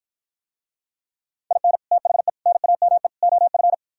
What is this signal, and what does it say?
Morse code, heard as a keyed tone, about 53 wpm, 700 Hz. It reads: IN THE NAME OF